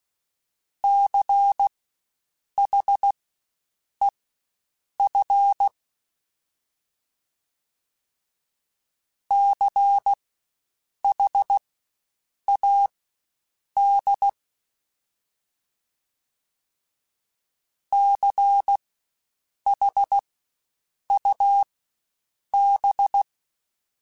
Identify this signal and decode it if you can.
Morse code, heard as a keyed tone, about 16 wpm, spaced as for 7 wpm, 800 Hz. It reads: CHEF CHAD CHUB
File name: Spaced Alphas / Spaced Alphas mp3s / Cs sp @15